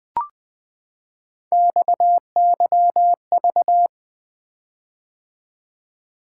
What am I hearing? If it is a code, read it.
Morse code, heard as a keyed tone, about 20 wpm, 700 Hz. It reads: XYV
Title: Three Letter-Number Combinations - 20wpm